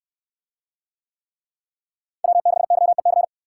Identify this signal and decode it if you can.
Morse code, heard as a keyed tone, about 67 wpm, 700 Hz. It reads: W6YEC